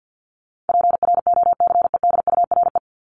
Morse code, heard as a keyed tone, about 50 wpm, 700 Hz. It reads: PROCEDURE
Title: Top 1751-2000 Words 50wpm